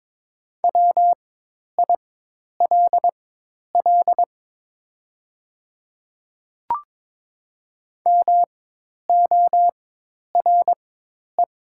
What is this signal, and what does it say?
Morse code, heard as a keyed tone, about 22 wpm, spaced as for 10 wpm, 700 Hz. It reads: WILL MORE
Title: 4x Character Spacing - Top 100 Words - 22wpm